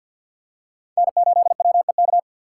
Morse code, heard as a keyed tone, about 53 wpm, 700 Hz. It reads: N8WEK
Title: USA Call Signs - 5 Characters - 3x Character Spacing - 50wpm